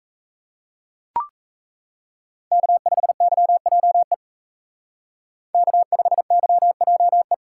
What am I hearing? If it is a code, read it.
Morse code, heard as a keyed tone, about 34 wpm, 700 Hz. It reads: K5YJE K5YJE
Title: Speed Racing - USA 1x3 Call Signs 28wpm